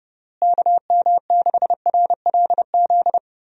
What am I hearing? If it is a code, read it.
Morse code, heard as a keyed tone, about 30 wpm, 700 Hz. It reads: KM6RLZ
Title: USA Call Signs - 6 Characters - Repeated 3x - 30wpm